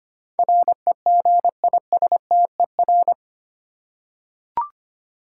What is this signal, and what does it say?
Morse code, heard as a keyed tone, about 25 wpm, 700 Hz. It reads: REGISTER
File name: Eight Letter Words - 25wpm